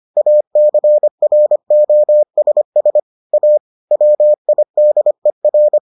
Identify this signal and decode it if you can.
Morse code, heard as a keyed tone, about 25 wpm, 600 Hz. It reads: ACROSS A WIDER